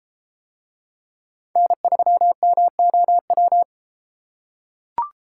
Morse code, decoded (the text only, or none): N3MOW